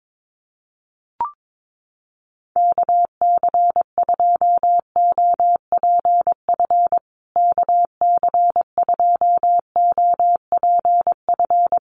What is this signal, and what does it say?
Morse code, heard as a keyed tone, about 22 wpm, 700 Hz. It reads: KC2OPF KC2OPF
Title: USA Call Signs Repeated 2x 22wpm